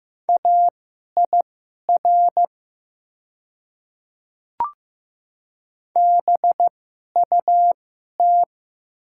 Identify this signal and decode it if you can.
Morse code, heard as a keyed tone, about 15 wpm, spaced as for 11 wpm, 700 Hz. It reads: AIR BUT
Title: 2x Character Spacing - Top 200 Words - 15wpm